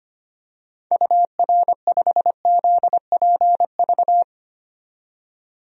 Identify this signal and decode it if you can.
Morse code, heard as a keyed tone, about 25 wpm, 700 Hz. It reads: UR5ZPV
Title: DXCC-100 Call Signs - 6 Characters - 25wpm